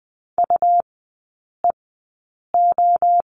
Morse code, heard as a keyed tone, about 20 wpm, spaced as for 8 wpm, 700 Hz. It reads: UEO
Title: 114 Lesson 19e - Letter w - Three Characters 20wpm